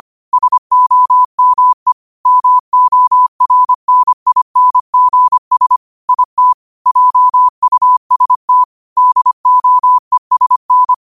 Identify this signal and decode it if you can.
Morse code, heard as a keyed tone, about 25 wpm, 1000 Hz. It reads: SOME MORNINGS IT JUST DOESN